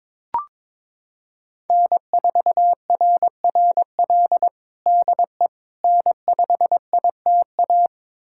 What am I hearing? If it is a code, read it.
Morse code, heard as a keyed tone, about 21 wpm, 700 Hz. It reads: N4RRL DE N5ITA
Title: USA Callsign DE Callsign - 5 Characters - 22wpm